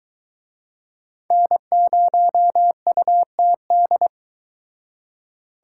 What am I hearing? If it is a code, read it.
Morse code, heard as a keyed tone, about 23 wpm, 700 Hz. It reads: N0UTD